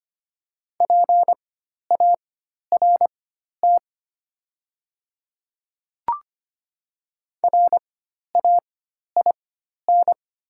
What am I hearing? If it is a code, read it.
Morse code, heard as a keyed tone, about 25 wpm, spaced as for 12 wpm, 700 Hz. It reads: PART RAIN
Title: 4x Character Spacing - Top 500 Words - 25wpm